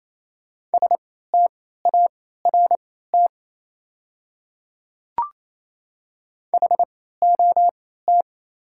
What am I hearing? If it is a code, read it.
Morse code, heard as a keyed tone, about 28 wpm, spaced as for 16 wpm, 690 Hz. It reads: START HOT
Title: 3x Character Spacing - Top 500 Words - 28wpm